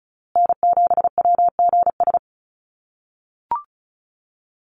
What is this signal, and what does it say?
Morse code, heard as a keyed tone, about 34 wpm, 700 Hz. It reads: N7WGS
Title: USA Call Signs Repeated 3x 35wpm